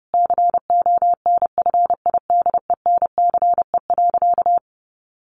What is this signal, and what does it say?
Morse code, heard as a keyed tone, about 30 wpm, 700 Hz. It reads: CONFIDENCE.
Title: Sentences from Top 1500 Words 30wpm